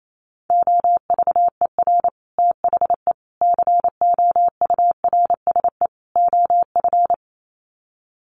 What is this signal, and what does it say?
Morse code, heard as a keyed tone, about 28 wpm, 700 Hz. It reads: OVER THE COURSE OF
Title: Sets of 4 Words 28wpm